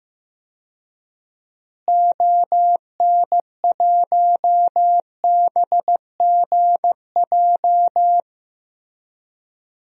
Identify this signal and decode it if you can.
Morse code, heard as a keyed tone, about 15 wpm, 700 Hz. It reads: ON1BGJ
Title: DXCC-100 Call Signs - 6 Characters - Repeated 2x - 15wpm